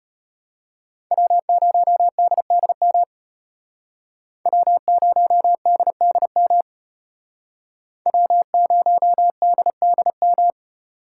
Morse code, heard as a keyed tone, about 32 wpm, 700 Hz. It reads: W0DDM W0DDM W0DDM